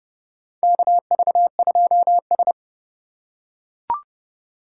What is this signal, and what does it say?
Morse code, heard as a keyed tone, about 32 wpm, 700 Hz. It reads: KV2S
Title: USA Call Signs - 2x Character Spacing - 30wpm